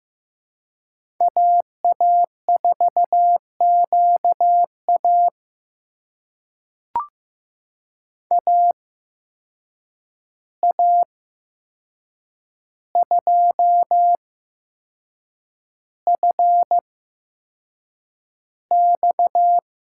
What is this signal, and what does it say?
Morse code, heard as a keyed tone, about 15 wpm, spaced as for 6 wpm, 700 Hz. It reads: AA4QA AA2FX
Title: USA Call Signs - 5 Characters - 8x Character Spacing - 15wpm